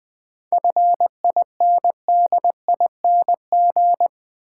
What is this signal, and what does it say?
Morse code, heard as a keyed tone, about 20 wpm, 700 Hz. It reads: FINDING
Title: Sentences from Top 1100 Words - New Word Drill Alt - 20wpm